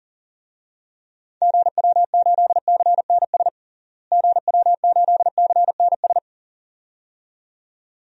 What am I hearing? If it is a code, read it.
Morse code, heard as a keyed tone, about 37 wpm, 700 Hz. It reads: GW8CNS GW8CNS
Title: DXCC-100 Call Signs - 6 Characters - Repeated 2x - 40wpm